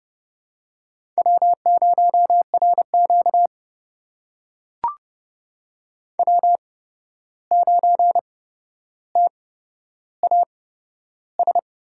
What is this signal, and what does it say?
Morse code, heard as a keyed tone, about 32 wpm, spaced as for 12 wpm, 700 Hz. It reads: W0RQ W9TAS
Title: USA Call Signs - 8x Character Spacing - 30wpm